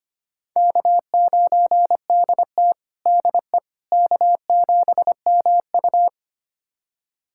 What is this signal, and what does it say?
Morse code, heard as a keyed tone, about 25 wpm, 700 Hz. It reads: K9DT DE K7MU
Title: USA Callsign DE Callsign - 4 Characters - 25wpm